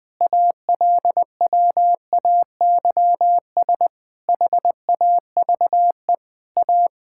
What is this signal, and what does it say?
Morse code, heard as a keyed tone, about 20 wpm, 700 Hz. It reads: ALWAYS HAVE A
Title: Sets of 9 Words - 20wpm